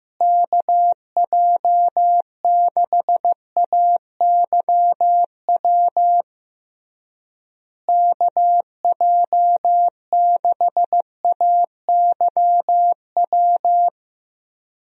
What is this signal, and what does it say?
Morse code, heard as a keyed tone, about 15 wpm, 700 Hz. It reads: KJ6AYW KJ6AYW
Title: USA Call Signs - 6 Characters - Repeated 3x - 15wpm